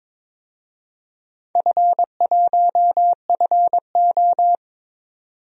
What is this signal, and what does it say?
Morse code, heard as a keyed tone, about 21 wpm, 700 Hz. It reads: F1FO
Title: DXCC-100 Call Signs - 4 Characters - 22wpm